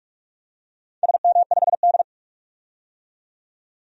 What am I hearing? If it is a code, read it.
Morse code, heard as a keyed tone, about 46 wpm, 700 Hz. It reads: SM5D
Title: DXCC-100 Call Signs - 4 Characters - 45wpm